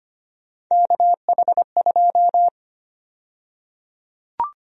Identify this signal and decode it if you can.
Morse code, heard as a keyed tone, about 25 wpm, 700 Hz. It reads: KH2
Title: USA Call Sign Prefixes - Repeated 2x - 25wpm